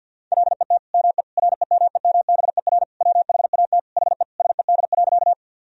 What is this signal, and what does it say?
Morse code, heard as a keyed tone, about 50 wpm, 700 Hz. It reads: LET ME REMEMBER WHAT HE SED.
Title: Sentences from Top 400 Words and Abbreviations 50wpm